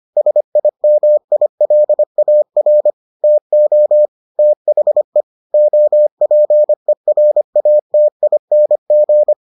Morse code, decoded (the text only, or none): SIMILAR TO THE OPERATING